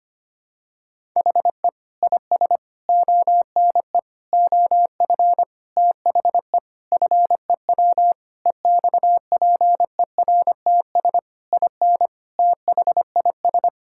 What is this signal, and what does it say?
Morse code, heard as a keyed tone, about 25 wpm, 700 Hz. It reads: HE IS ONE OF THE FEW EXPERTS IN THIS
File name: practice / 25wpm / Sentences with 4-Word Phrases - 25wpm